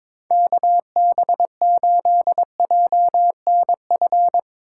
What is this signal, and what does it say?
Morse code, heard as a keyed tone, about 22 wpm, 700 Hz. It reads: KB8JNF
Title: USA Call Signs - 6 Characters - 2x Character Spacing - 22wpm